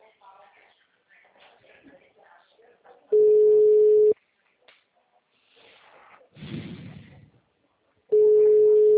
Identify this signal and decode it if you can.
Ringback tone (Europe, 425 Hz), with speech.